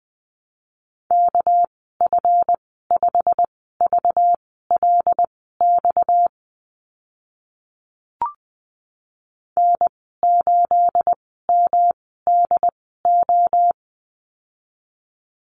Morse code, heard as a keyed tone, about 20 wpm, spaced as for 15 wpm, 700 Hz. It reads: KF5VLX N8MDO
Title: 238 Lesson 38g - Letter q - Call Signs 20wpm